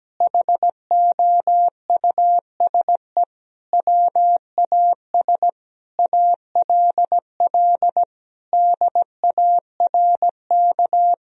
Morse code, heard as a keyed tone, about 17 wpm, 700 Hz. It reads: HOUSE WAS ALL DARK
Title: Sentences from Top 500 Words and Abbreviations 17wpm